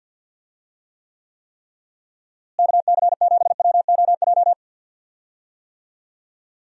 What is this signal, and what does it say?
Morse code, heard as a keyed tone, about 51 wpm, 700 Hz. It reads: KC7WOJ
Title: USA Call Signs - 6 Characters - Repeated 3x - 50wpm